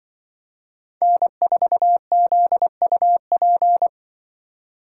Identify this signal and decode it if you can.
Morse code, heard as a keyed tone, about 25 wpm, 700 Hz. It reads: N4ZUP
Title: Speed Racing - USA 1x3 Call Signs 20wpm